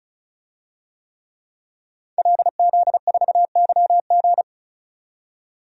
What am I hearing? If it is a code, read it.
Morse code, heard as a keyed tone, about 35 wpm, 700 Hz. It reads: LZ4YG
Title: DXCC-100 Call Signs - 5 Characters - 35wpm